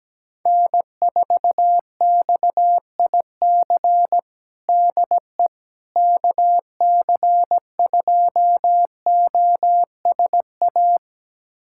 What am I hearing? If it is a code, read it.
Morse code, heard as a keyed tone, about 17 wpm, 700 Hz. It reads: N4XIC DE KC2OSA
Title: USA Callsign DE Callsign 17wpm